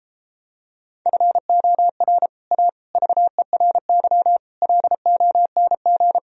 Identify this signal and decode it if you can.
Morse code, heard as a keyed tone, about 33 wpm, 700 Hz. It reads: FOR A VERY LONG